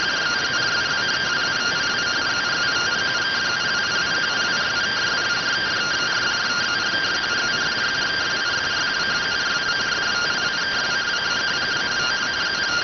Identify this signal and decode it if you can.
Two-tone teleprinter signal, about 170 Hz shift.